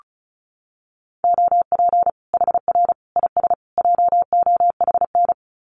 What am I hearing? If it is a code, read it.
Morse code, heard as a keyed tone, about 34 wpm, 700 Hz. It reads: OP HR IS JOHN
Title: Common QSO Phrases 35wpm